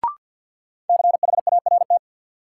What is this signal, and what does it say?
Morse code, heard as a keyed tone, about 53 wpm, 700 Hz. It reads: CHART
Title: Sentences from Top 1300 Words - New Word Drill - 50wpm